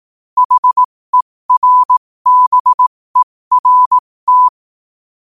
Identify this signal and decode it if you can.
Morse code, heard as a keyed tone, about 18 wpm, spaced as for 15 wpm, 1000 Hz. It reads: HERBERT
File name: qotdfiles / 15wpm